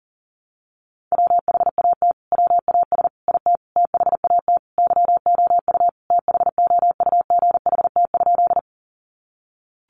Morse code, heard as a keyed tone, about 40 wpm, 700 Hz. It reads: WHAT WAS IT THAT YOU THOUGHT?
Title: Sentences from Top 300 Words 40wpm